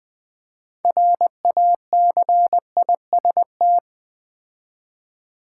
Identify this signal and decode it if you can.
Morse code, heard as a keyed tone, about 20 wpm, 700 Hz. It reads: RACIST
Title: Build A Word - 3 Letters At A Time - 6-Letter Words - 20wpm